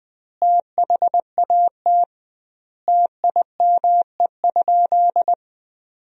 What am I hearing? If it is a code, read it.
Morse code, heard as a keyed tone, about 20 wpm, 700 Hz. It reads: THAT TIME?
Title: 2x Word-Spacing - Sentences from Top 200 Words - 20wpm